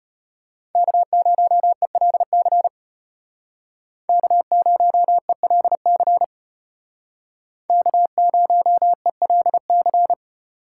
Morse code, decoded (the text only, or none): K0ELC K0ELC K0ELC